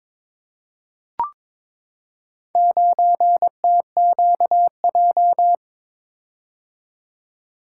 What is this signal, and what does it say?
Morse code, heard as a keyed tone, about 22 wpm, 700 Hz. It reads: 9TQJ
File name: practice / 22wpm / Four Letter-Number Combinations - 22wpm